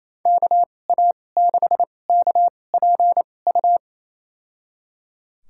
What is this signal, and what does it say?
Morse code, heard as a keyed tone, about 28 wpm, spaced as for 20 wpm, 700 Hz. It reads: KA6KPU
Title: USA Call Signs - 2x Character Spacing - 28wpm